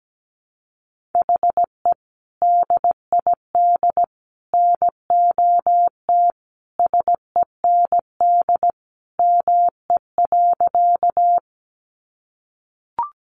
Morse code, heard as a keyed tone, about 17 wpm, 700 Hz. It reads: HE DID NOT SEND ME.
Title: Sentences from Top 1500 Words 17wpm